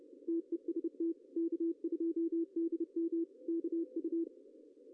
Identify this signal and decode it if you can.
Morse code, heard as a keyed tone, about 29 wpm, 330 Hz, with radio noise. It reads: TEST K2DM KU
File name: CW / cqww-cw-k5tr